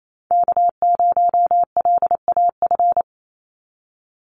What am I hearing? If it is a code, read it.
Morse code, heard as a keyed tone, about 28 wpm, 700 Hz. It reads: K0LAF
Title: NA SOTA Chaser Callsigns 28wpm